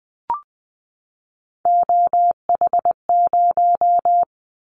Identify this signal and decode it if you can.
Morse code, heard as a keyed tone, about 20 wpm, 700 Hz. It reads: OH0
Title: Callsign Prefixes for Top 100 DXCC - 20wpm